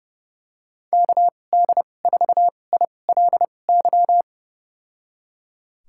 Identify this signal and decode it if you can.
Morse code, heard as a keyed tone, about 32 wpm, spaced as for 22 wpm, 700 Hz. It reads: KD4ILY
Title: USA Call Signs - 6 Characters - 2x Character Spacing - 30wpm